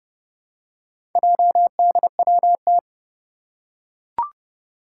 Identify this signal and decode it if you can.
Morse code, heard as a keyed tone, about 30 wpm, 700 Hz. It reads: JDWT
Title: Four Challenging Letter-Number Combinations 30wpm